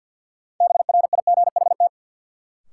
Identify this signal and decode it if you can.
Morse code, heard as a keyed tone, about 52 wpm, 700 Hz. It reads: BRIGHT
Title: Sentences from Top 1000 Words - New Word Drill - 50wpm